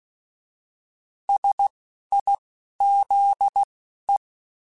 Morse code, heard as a keyed tone, about 16 wpm, spaced as for 11 wpm, 800 Hz. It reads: SIZE
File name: Technical Subjects / Technical Subjects-mp3s / Sw Pwr Sup @10